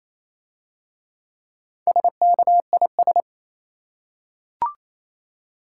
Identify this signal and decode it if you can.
Morse code, heard as a keyed tone, about 27 wpm, 700 Hz. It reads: SKIS
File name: Easy Words - Speed-Racing with Hints - 22wpm